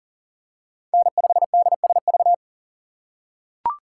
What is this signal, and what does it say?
Morse code, heard as a keyed tone, about 40 wpm, 700 Hz. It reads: N5DSV
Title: USA Call Signs - 5 Characters - 2x Character Spacing - 40wpm